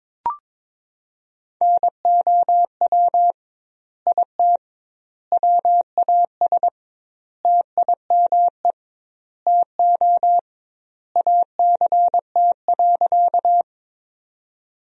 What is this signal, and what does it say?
Morse code, heard as a keyed tone, about 21 wpm, 700 Hz. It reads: NOW IT WAS TIME TO ACT.
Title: 2x Word-Spacing - Sentences from Top 200 Words - 22wpm